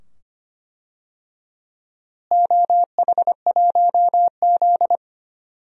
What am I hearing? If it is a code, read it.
Morse code, heard as a keyed tone, about 25 wpm, 700 Hz. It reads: OH1Z
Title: DXCC-100 Call Signs - 4 Characters - Repeated 2x - 25wpm